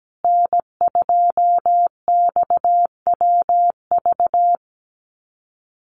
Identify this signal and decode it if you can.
Morse code, heard as a keyed tone, about 17 wpm, 700 Hz. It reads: N2XWV
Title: USA Call Signs Repeated 3x 17wpm